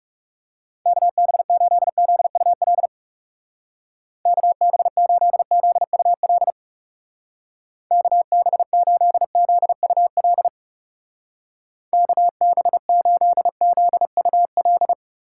Morse code, heard as a keyed tone, about 36 wpm, 700 Hz. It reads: KB8ZUL KB8ZUL KB8ZUL KB8ZUL
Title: Speed Racing - USA 2x3 Call Signs 30wpm